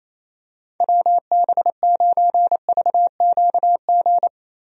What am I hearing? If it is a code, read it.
Morse code, heard as a keyed tone, about 27 wpm, 700 Hz. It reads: WB9VQG
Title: USA Call Signs - 6 Characters - Repeated 2x - 28wpm